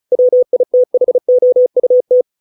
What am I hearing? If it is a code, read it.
Morse code, heard as a keyed tone, about 34 wpm, 500 Hz. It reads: WITHOUT